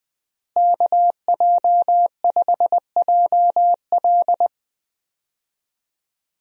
Morse code, heard as a keyed tone, about 20 wpm, 700 Hz. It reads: KJ5JL
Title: USA Call Signs - 5 Characters - Repeated 3x - 20wpm